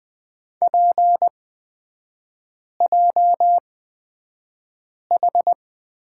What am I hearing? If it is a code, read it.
Morse code, heard as a keyed tone, about 20 wpm, spaced as for 5 wpm, 700 Hz. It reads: PJH
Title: 255 Lesson 41e - Period - Three Characters 20wpm